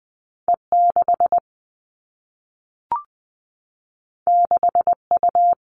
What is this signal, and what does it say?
Morse code, heard as a keyed tone, about 20 wpm, 700 Hz. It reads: E 6 6 U
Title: All Cut Numbers - 2-Digit 20wpm